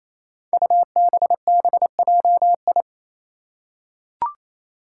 Morse code, heard as a keyed tone, about 28 wpm, 700 Hz. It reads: UBBJI
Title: Five Challenging Letter-Number Combinations 28wpm